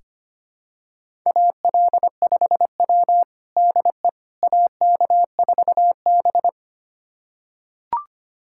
Morse code, heard as a keyed tone, about 25 wpm, 700 Hz. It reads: AL5W DE AK4B
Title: USA Callsign DE Callsign - 4 Characters - 25wpm